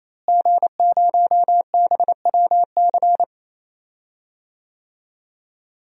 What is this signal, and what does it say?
Morse code, heard as a keyed tone, about 28 wpm, 700 Hz. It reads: G0BWC
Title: DXCC-100 Call Signs - 5 Characters - Repeated 3x - 28wpm